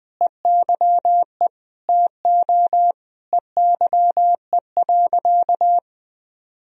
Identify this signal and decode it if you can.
Morse code, heard as a keyed tone, about 21 wpm, 700 Hz. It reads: EYE TO EYE.
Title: Sentences from Top 1000 Words - Review Part 2 - 20wpm